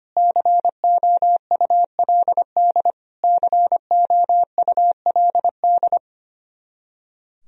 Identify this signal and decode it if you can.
Morse code, heard as a keyed tone, about 25 wpm, 700 Hz. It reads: COULD COULD
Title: Top 100 Words Repeated 25wpm